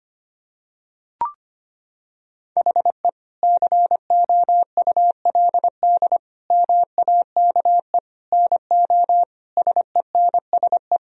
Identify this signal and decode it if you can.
Morse code, heard as a keyed tone, about 25 wpm, 700 Hz. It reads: HE COULD MAKE NO SENSE
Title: Sentences from Top 1000 Words - Review Part 4 - 25wpm